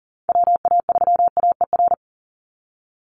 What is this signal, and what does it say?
Morse code, heard as a keyed tone, about 40 wpm, 700 Hz. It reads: WA3AER
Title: Call Signs Repeated 3x - 40wpm